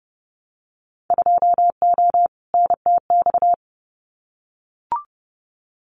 Morse code, heard as a keyed tone, about 30 wpm, 700 Hz. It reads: 2O NTX
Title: Winter Field Day Exchanges Encore 30wpm